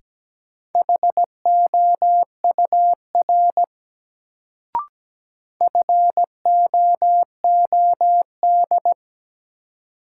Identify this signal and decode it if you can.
Morse code, heard as a keyed tone, about 17 wpm, 700 Hz. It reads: HOUR FOOD
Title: Sets of 5 Words - New Word Drill - 17wpm